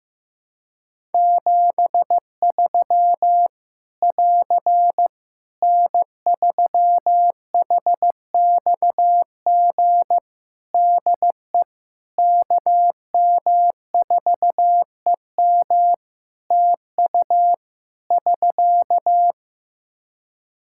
Morse code, heard as a keyed tone, about 15 wpm, 700 Hz. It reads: 73 <AR> N3HXG DE KM4EM TU <SK>
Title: Practice QSO - Exchange 4 - Ending - 15wpm